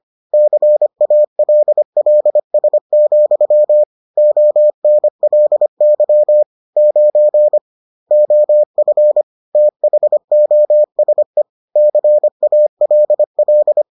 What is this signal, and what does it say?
Morse code, heard as a keyed tone, about 25 wpm, 600 Hz. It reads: CALLS, ONLY 9 OF THOSE CALL